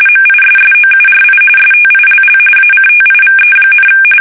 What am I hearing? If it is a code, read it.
RTTY, about 45.45 baud. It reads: MISSOURI STILL ACTIVE,
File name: fsk850 / battleshipnj